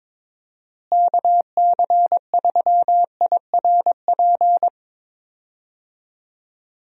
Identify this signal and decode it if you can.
Morse code, heard as a keyed tone, about 21 wpm, 700 Hz. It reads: KC3IRP
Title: USA Call Signs - 6 Characters - Repeated 3x - 22wpm